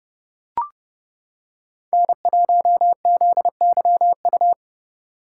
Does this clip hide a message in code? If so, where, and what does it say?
Yes: Morse code from 1.9 to 4.5 s, heard as a keyed tone, about 29 wpm, 700 Hz. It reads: N1ZYU